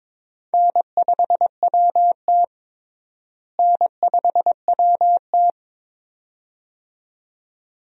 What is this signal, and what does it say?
Morse code, heard as a keyed tone, about 22 wpm, 700 Hz. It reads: N5WT N5WT
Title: USA Call Signs - 4 Characters - Repeated 2x - 22wpm